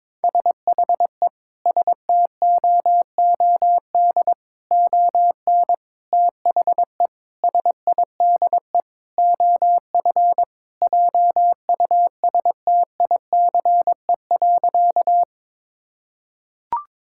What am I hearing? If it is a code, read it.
Morse code, heard as a keyed tone, about 22 wpm, 700 Hz. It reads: SHE STOOD ON THE SIDE OF JUSTICE.
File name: practice / 22wpm / Sentences with 4-Word Phrases - 22wpm